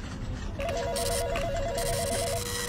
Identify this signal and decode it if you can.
There is no code.